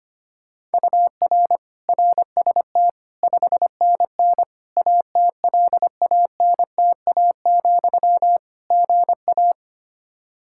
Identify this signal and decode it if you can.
Morse code, heard as a keyed tone, about 25 wpm, 700 Hz. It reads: UR RST 5NN ATLANTA, GA
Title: Practice QSO - Exchange 1 - Beginning - 25wpm